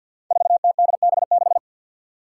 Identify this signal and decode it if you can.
Morse code, heard as a keyed tone, about 47 wpm, 700 Hz. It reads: 4TDB6
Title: Five Challenging Letter-Number Combinations 50wpm